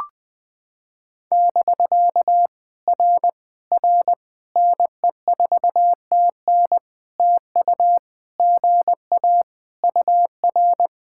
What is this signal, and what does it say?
Morse code, heard as a keyed tone, about 20 wpm, 700 Hz. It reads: <BK> R R NE4TN TU GA UR
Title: POTA QSO Practice - 20wpm